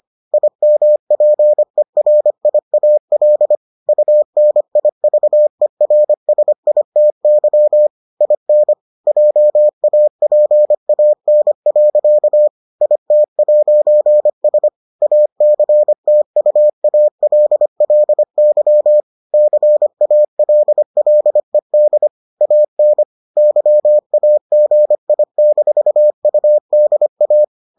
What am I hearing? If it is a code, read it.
Morse code, heard as a keyed tone, about 25 wpm, 600 Hz. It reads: IMPERIAL UNIVERSITY IN JAPAN. IT'S ACTUALLY CALLED AN YAGI-UDA